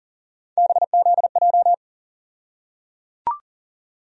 Morse code, heard as a keyed tone, about 37 wpm, 700 Hz. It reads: BZJ
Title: USA Call Sign Postfixes - 3 Letters 40wpm